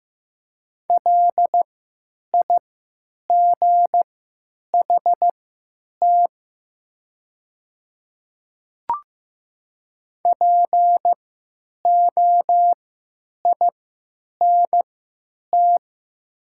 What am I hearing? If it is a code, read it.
Morse code, heard as a keyed tone, about 15 wpm, spaced as for 8 wpm, 700 Hz. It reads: LIGHT POINT
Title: 3x Character Spacing - Top 300 Words - 15wpm